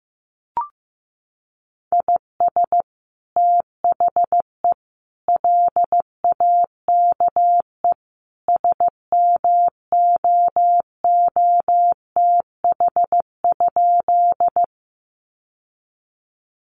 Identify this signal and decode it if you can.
Morse code, heard as a keyed tone, about 15 wpm, 700 Hz. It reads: IS THE LAKE SMOOTH?
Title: Sentences from Top 1400 Words 15wpm